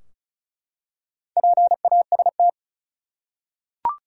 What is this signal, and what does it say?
Morse code, heard as a keyed tone, about 34 wpm, 700 Hz. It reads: PAST
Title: Four Letter Words - 35wpm